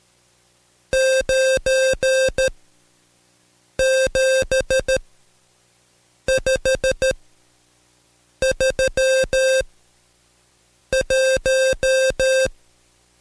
Morse code, heard as a keyed tone, about 13 wpm, spaced as for 5 wpm, 530 Hz. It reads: 97531
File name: Code Quick / AudioDownld2 / Seg107-Day13-DayDrill 13